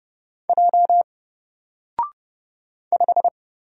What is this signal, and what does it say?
Morse code, heard as a keyed tone, about 30 wpm, 700 Hz. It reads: J 5
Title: Single Letter-Number 30wpm